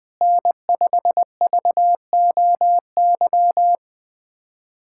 Morse code, heard as a keyed tone, about 20 wpm, 700 Hz. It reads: N5VOY
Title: USA Callsign DE Callsign - 5 Characters - 20wpm